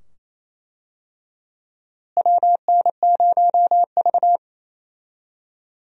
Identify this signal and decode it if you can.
Morse code, heard as a keyed tone, about 28 wpm, 700 Hz. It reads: WN0V
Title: USA Call Signs - 4 Characters - Repeated 2x - 28wpm